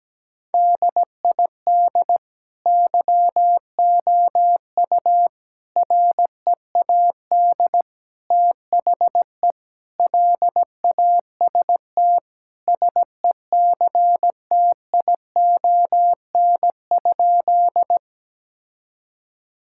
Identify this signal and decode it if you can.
Morse code, heard as a keyed tone, about 17 wpm, 700 Hz. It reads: DID YOU READ THE LAST SECTION?
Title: Sentences from Top 1000 Words - Review Part 5 - 17wpm